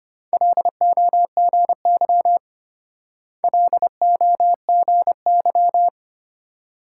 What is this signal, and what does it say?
Morse code, heard as a keyed tone, about 32 wpm, 700 Hz. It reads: LOGY LOGY